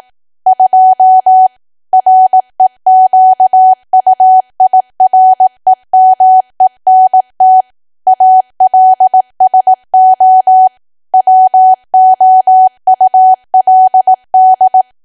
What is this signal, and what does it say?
Morse code, heard as a keyed tone, about 18 wpm, 750 Hz. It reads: 2 REQUIREMENT ALSO WOULD
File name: cwa / W1AW-Level2 / 18WPM